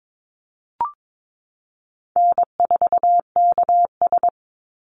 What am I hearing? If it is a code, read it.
Morse code, heard as a keyed tone, about 22 wpm, 700 Hz. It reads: N4KS
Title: Easy Call Signs Repeated 2x - 22wpm